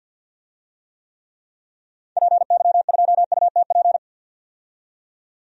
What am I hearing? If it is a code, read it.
Morse code, heard as a keyed tone, about 49 wpm, 700 Hz. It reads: PY2UTP
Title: DXCC-100 Call Signs - 6 Characters - 50wpm